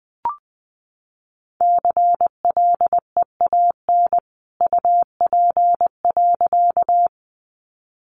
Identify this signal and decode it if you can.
Morse code, heard as a keyed tone, about 20 wpm, 700 Hz. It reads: CLEAN UP.
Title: A Sound Of Thunder 20wpm